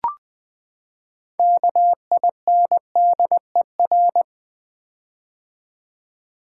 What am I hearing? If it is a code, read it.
Morse code, heard as a keyed tone, about 20 wpm, 700 Hz. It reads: KINDER
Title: Build A Word - 6 Letters At A Time - 12-Letter Words - 20wpm